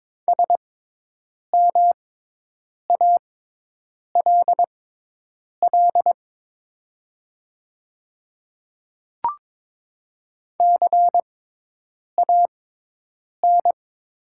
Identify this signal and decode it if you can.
Morse code, heard as a keyed tone, about 22 wpm, spaced as for 8 wpm, 700 Hz. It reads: SMALL CAN
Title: 6x Character Spacing - Top 200 Words - 22wpm